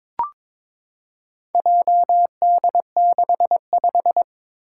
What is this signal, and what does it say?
Morse code, heard as a keyed tone, about 22 wpm, 700 Hz. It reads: JD65
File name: practice / 22wpm / Four Letter-Number Combinations - 22wpm